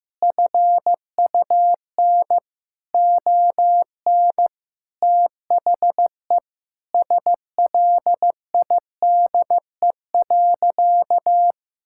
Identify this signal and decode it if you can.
Morse code, heard as a keyed tone, about 15 wpm, 700 Hz. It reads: FUN ON THE SLIDE.
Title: Sentences from Top 1600 Words 15wpm